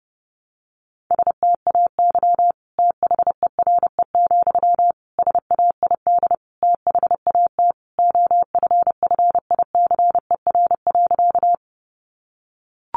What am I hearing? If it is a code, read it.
Morse code, heard as a keyed tone, about 30 wpm, 700 Hz. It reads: STAY THERE, SAID THAT OFFICER.
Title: Sentences from Top 1200 Words 30wpm